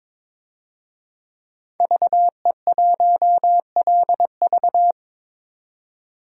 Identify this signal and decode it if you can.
Morse code, heard as a keyed tone, about 22 wpm, 700 Hz. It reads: VE1LV
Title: DXCC-100 Call Signs - 5 Characters - 22wpm